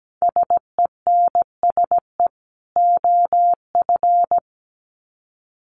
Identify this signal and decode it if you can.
Morse code, heard as a keyed tone, about 17 wpm, 700 Hz. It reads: SENSE OF
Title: Sets of 2 Words 17wpm